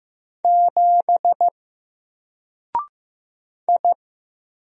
Morse code, heard as a keyed tone, about 15 wpm, 700 Hz. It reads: 7 I
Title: Single Letter-Number Alt 15wpm